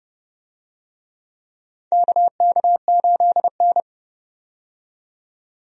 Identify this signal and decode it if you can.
Morse code, heard as a keyed tone, about 29 wpm, 700 Hz. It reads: KK8N
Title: USA Call Signs - 4 Characters - 30wpm